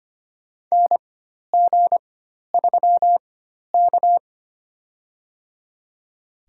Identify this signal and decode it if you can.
Morse code, heard as a keyed tone, about 25 wpm, spaced as for 12 wpm, 700 Hz. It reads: NG3K
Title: USA Call Signs - 4x Character Spacing - 25wpm